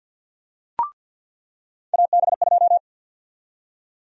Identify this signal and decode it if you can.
Morse code, heard as a keyed tone, about 48 wpm, 700 Hz. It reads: AB2
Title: USA Call Sign Prefixes - 50wpm